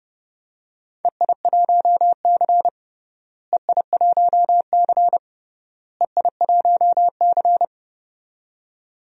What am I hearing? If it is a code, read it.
Morse code, heard as a keyed tone, about 30 wpm, 700 Hz. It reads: EI1C EI1C EI1C
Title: DXCC-100 Call Signs - 4 Characters - Repeated 3x - 30wpm